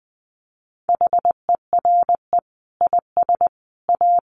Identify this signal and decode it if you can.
Morse code, heard as a keyed tone, about 20 wpm, 700 Hz. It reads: HERE IS A
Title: Sentences from Top 1200 Words 20wpm